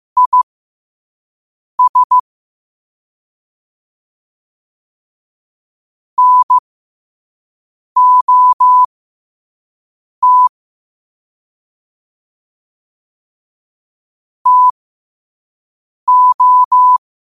Morse code, heard as a keyed tone, about 15 wpm, spaced as for 5 wpm, 1000 Hz. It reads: IS NOT TO